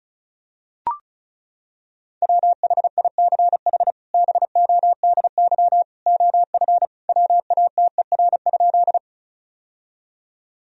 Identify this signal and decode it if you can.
Morse code, heard as a keyed tone, about 36 wpm, 700 Hz. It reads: WHICH BODY OF WATER?
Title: Sentences from Top 1000 Words - Review Part 2 - 35wpm